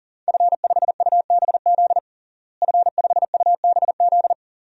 Morse code, heard as a keyed tone, about 40 wpm, 700 Hz. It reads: F5UBZ F5UBZ
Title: DXCC-100 Call Signs - 5 Characters - Repeated 3x - 40wpm